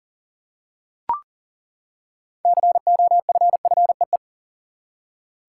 Morse code, heard as a keyed tone, about 40 wpm, 700 Hz. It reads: COFFEE
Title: Six Letter Words - 40wpm